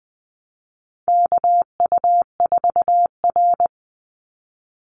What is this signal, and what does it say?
Morse code, heard as a keyed tone, about 20 wpm, 690 Hz. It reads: KU4R
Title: NA SOTA Chaser Callsigns 20wpm